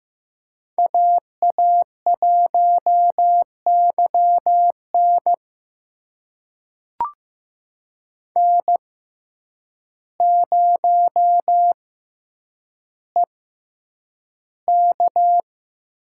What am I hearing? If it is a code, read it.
Morse code, heard as a keyed tone, about 15 wpm, spaced as for 8 wpm, 700 Hz. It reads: AA1YN N0EK
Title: USA Call Signs - 6x Character Spacing - 15wpm